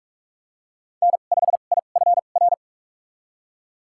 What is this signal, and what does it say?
Morse code, heard as a keyed tone, about 46 wpm, spaced as for 33 wpm, 700 Hz. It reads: N5IFR
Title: USA Call Signs - 2x Character Spacing - 45wpm